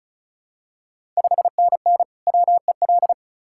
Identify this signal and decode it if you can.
Morse code, heard as a keyed tone, about 36 wpm, 690 Hz. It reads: 5NN WEL
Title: Ontario QSO Party Exchange 35wpm